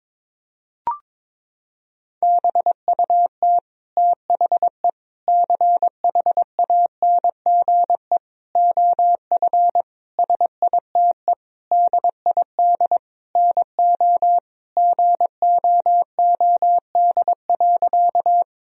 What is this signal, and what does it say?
Morse code, heard as a keyed tone, about 22 wpm, 700 Hz. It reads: BUT THE CHANGE OF SITE DID NO GOOD.
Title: Sentences from Top 1000 Words - Review Part 4 - 22wpm